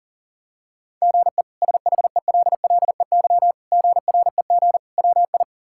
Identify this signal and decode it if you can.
Morse code, heard as a keyed tone, about 40 wpm, 700 Hz. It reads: GE SHELLEY GREG WI
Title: K1USN SST QSO Practice - 40wpm